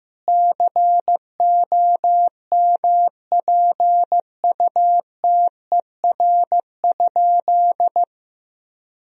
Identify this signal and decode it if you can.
Morse code, heard as a keyed tone, about 15 wpm, 700 Hz. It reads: COMPUTER?
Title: Sentences from Top 1000 Words - Review Part 3 - 15wpm